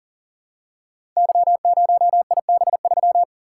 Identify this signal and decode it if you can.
Morse code, heard as a keyed tone, about 42 wpm, 700 Hz. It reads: Y0IB3
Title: Code Groups - 5 Letter:Numbers - 40wpm